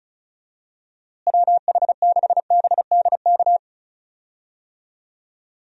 Morse code, heard as a keyed tone, about 36 wpm, 700 Hz. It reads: WH6BDK
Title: USA Call Signs - 6 Characters - Repeated 2x - 35wpm